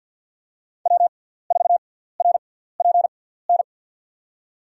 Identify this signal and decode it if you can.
Morse code, heard as a keyed tone, about 50 wpm, spaced as for 17 wpm, 690 Hz. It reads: W4RPN
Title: USA Call Signs - 5 Characters - 6x Character Spacing - 50wpm